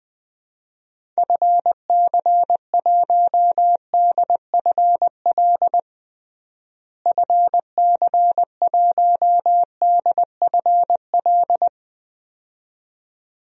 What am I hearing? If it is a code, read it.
Morse code, heard as a keyed tone, about 20 wpm, 700 Hz. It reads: FC1DFL FC1DFL
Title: DXCC-100 Call Signs - 6 Characters - Repeated 2x - 20wpm